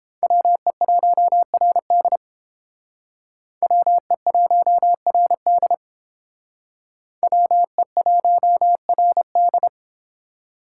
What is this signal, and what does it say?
Morse code, heard as a keyed tone, about 28 wpm, 700 Hz. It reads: WE1RD WE1RD WE1RD